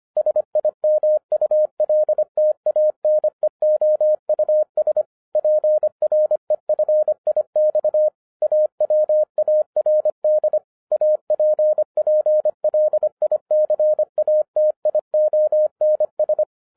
Morse code, heard as a keyed tone, about 24 wpm, 610 Hz. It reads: SIMULTANEOUS PREFIX AWARD APPLICATIONS
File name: 2012-01-mp3 / n14-FUN WITH NUMBERS-25wpm